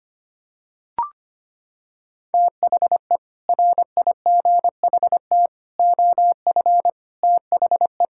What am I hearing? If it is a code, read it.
Morse code, heard as a keyed tone, about 25 wpm, 700 Hz. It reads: THE RIGHT OF THE
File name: practice / 25wpm / Sentences with 9-Word Phrases - 25wpm